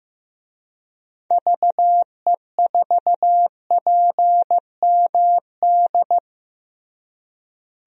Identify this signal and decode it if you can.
Morse code, heard as a keyed tone, about 15 wpm, 700 Hz. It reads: VE4PMD
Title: DXCC-100 Call Signs - 6 Characters - Repeated 3x - 15wpm